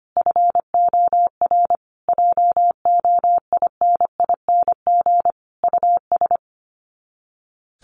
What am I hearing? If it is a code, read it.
Morse code, heard as a keyed tone, about 25 wpm, 700 Hz. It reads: FOR JOINING US
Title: Sets of 5 Words 25wpm